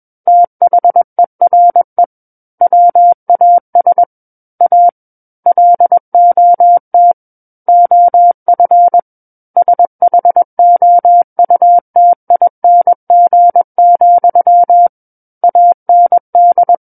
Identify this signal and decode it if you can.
Morse code, heard as a keyed tone, about 21 wpm, 700 Hz. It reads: THERE WAS A LOT OF SHOUTING, AND
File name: Book One / 12-Chapter 12 - 21 WPM